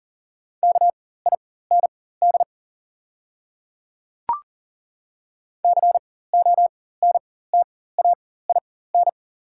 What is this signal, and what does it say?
Morse code, heard as a keyed tone, about 40 wpm, spaced as for 19 wpm, 690 Hz. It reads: KIND CONTAIN
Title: 4x Character Spacing - Top 500 Words - 40wpm